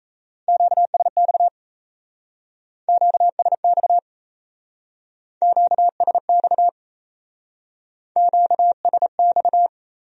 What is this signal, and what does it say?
Morse code, heard as a keyed tone, about 36 wpm, 700 Hz. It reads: QSX QSX QSX QSX